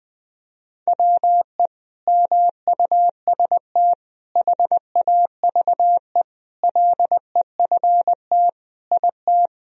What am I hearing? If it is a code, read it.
Morse code, heard as a keyed tone, about 20 wpm, 700 Hz. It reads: WE MUST HAVE LEFT IT
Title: Sentences from Top 1000 Words - Review Part 2 - 20wpm